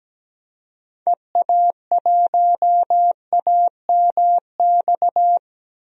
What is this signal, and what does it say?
Morse code, heard as a keyed tone, about 17 wpm, 700 Hz. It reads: EA1AMX
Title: DXCC-100 Call Signs - 6 Characters - 17wpm